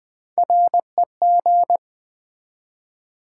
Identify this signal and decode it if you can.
Morse code, heard as a keyed tone, about 20 wpm, 700 Hz. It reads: REG
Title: Build A Word - 3 Letters At A Time - 8-Letter Words - 20wpm